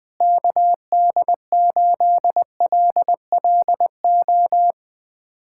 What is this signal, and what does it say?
Morse code, heard as a keyed tone, about 21 wpm, 700 Hz. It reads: KD8LLO
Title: USA Callsign DE Callsign - 6 Characters - 20wpm